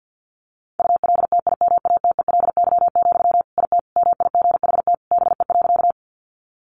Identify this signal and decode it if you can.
Morse code, heard as a keyed tone, about 50 wpm, 700 Hz. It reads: ULTIMATELY, IT MIGHT BE.
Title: Sentences from Top 1200 Words 50wpm